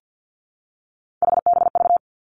Morse code, heard as a keyed tone, about 50 wpm, 700 Hz. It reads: HBV
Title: Three Challenging Letter-Number Combinations 50wpm